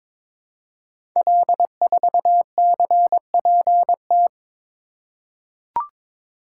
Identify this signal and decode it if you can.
Morse code, heard as a keyed tone, about 22 wpm, 700 Hz. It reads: L4CPT
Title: Code Groups - 5 Letter:Numbers - 22wpm